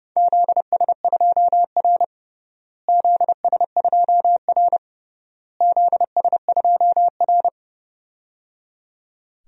Morse code, heard as a keyed tone, about 30 wpm, 700 Hz. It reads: ZS2R ZS2R ZS2R
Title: DXCC-100 Call Signs - 4 Characters - Repeated 3x - 30wpm